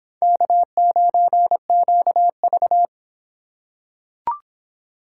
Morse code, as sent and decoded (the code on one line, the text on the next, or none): -.- ----. --.- ...-
K9QV